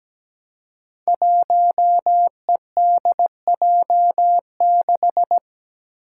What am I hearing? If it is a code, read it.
Morse code, heard as a keyed tone, about 17 wpm, 700 Hz. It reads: 1EDJ6
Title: Five Challenging Letter-Number Combinations 17wpm